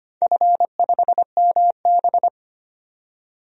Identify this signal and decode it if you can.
Morse code, heard as a keyed tone, about 25 wpm, 700 Hz. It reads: F5MB
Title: DXCC-100 Call Signs - 4 Characters - 25wpm